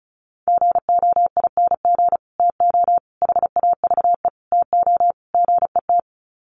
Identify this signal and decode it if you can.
Morse code, heard as a keyed tone, about 34 wpm, 700 Hz. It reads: GOING TO HAVE TO GET
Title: Sets of 5 Words Encore2 35wpm